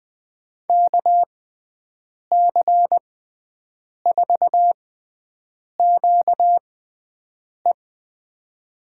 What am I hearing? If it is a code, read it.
Morse code, heard as a keyed tone, about 20 wpm, spaced as for 7 wpm, 700 Hz. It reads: KC4QE
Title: USA Call Signs - 5 Characters - 6x Character Spacing - 20wpm